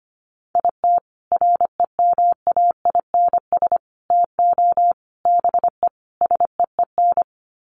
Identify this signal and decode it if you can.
Morse code, heard as a keyed tone, about 25 wpm, 700 Hz. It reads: IT REMAINS TO BE SEEN
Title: Sets of 5 Words Encore 25wpm